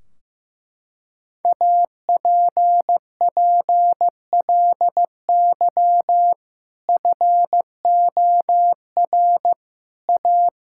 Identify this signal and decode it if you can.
Morse code, heard as a keyed tone, about 15 wpm, 700 Hz. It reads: APPLY FOR A
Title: Sentences from Top 1600 Words 15wpm